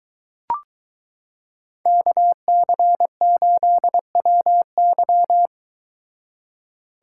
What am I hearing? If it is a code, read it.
Morse code, heard as a keyed tone, about 22 wpm, 700 Hz. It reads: KC8WY